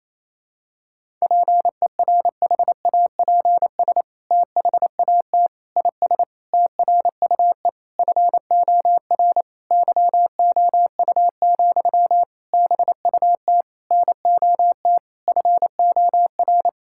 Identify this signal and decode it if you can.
Morse code, heard as a keyed tone, about 28 wpm, 700 Hz. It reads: PERHAPS THAT IS TRUE FOR YOU, BUT NOT FOR
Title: Sentences from Top 1000 Words - Review Part 5 - 28wpm